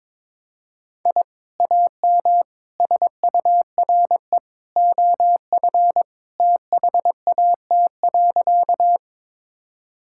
Morse code, heard as a keyed tone, about 22 wpm, 700 Hz. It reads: I AM SURE OF THAT.
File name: practice / 22wpm / Sentences with 2-Word Phrases - 22wpm